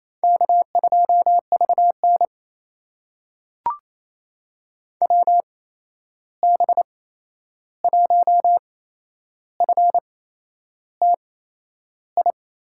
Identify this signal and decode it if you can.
Morse code, heard as a keyed tone, about 28 wpm, spaced as for 11 wpm, 700 Hz. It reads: K2VN WB1FTI
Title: USA Call Signs - 8x Character Spacing - 28wpm